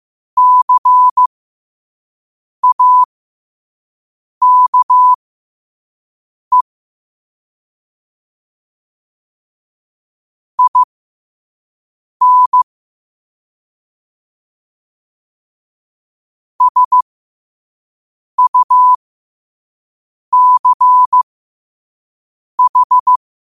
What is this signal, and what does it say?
Morse code, heard as a keyed tone, about 15 wpm, spaced as for 5 wpm, 1000 Hz. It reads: CAKE IN SUCH
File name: qotdfiles / 5wpm